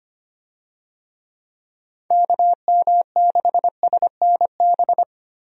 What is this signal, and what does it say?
Morse code, heard as a keyed tone, about 25 wpm, 700 Hz. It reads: KM6SNB
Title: USA Call Signs - 6 Characters - Repeated 2x - 25wpm